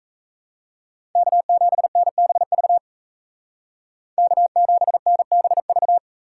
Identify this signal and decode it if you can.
Morse code, heard as a keyed tone, about 33 wpm, 700 Hz. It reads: K7NBV K7NBV
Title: Speed Racing - USA 1x3 Call Signs 28wpm